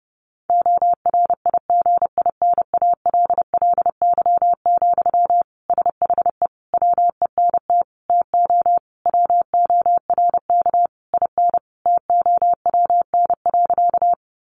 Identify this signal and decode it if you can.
Morse code, heard as a keyed tone, about 30 wpm, 700 Hz. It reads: ORIGINALLY, SHE WENT TO WORK IN TOWN.
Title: Sentences from Top 1500 Words 30wpm